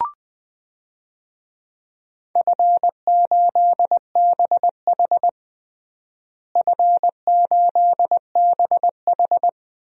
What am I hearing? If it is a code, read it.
Morse code, heard as a keyed tone, about 21 wpm, 700 Hz. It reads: F8BH F8BH
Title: DXCC-100 Call Signs - 4 Characters - Repeated 3x - 20wpm